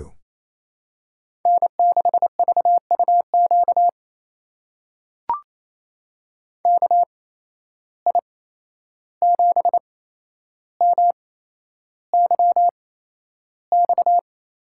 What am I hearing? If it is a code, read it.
Morse code, heard as a keyed tone, about 28 wpm, spaced as for 11 wpm, 700 Hz. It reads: N6VUQ KI7MYX
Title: USA Call Signs - 8x Character Spacing - 28wpm